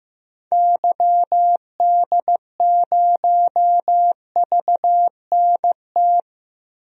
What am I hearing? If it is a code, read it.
Morse code, heard as a keyed tone, about 15 wpm, 700 Hz. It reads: YD0VNT
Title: DXCC-100 Call Signs - 6 Characters - 15wpm